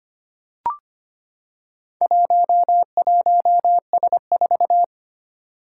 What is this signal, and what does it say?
Morse code, heard as a keyed tone, about 25 wpm, 700 Hz. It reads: 11S4
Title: Four Challenging Letter-Number Combinations 25wpm